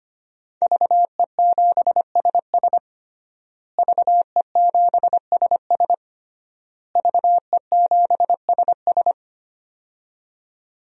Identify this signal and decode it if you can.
Morse code, heard as a keyed tone, about 25 wpm, 700 Hz. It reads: VE7SS VE7SS VE7SS
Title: DXCC-100 Call Signs - 5 Characters - Repeated 3x - 25wpm